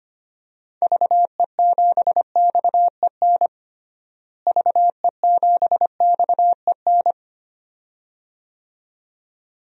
Morse code, heard as a keyed tone, about 25 wpm, 700 Hz. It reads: VE7XEN VE7XEN
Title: DXCC-100 Call Signs - 6 Characters - Repeated 2x - 25wpm